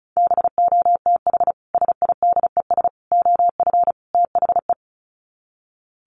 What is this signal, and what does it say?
Morse code, heard as a keyed tone, about 34 wpm, 700 Hz. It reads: BOTH SIDES OF THE
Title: Sets of 4 Words 35wpm